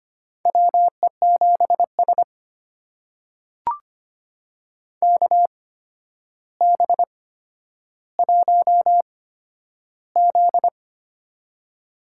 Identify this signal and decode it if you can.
Morse code, heard as a keyed tone, about 25 wpm, spaced as for 11 wpm, 700 Hz. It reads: WE7S KB1Z
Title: USA Call Signs - 4 Characters - 8x Character Spacing - 25wpm